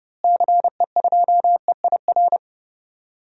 Morse code, heard as a keyed tone, about 31 wpm, 700 Hz. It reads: CE2EIR
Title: DXCC-100 Call Signs - 6 Characters - Repeated 3x - 30wpm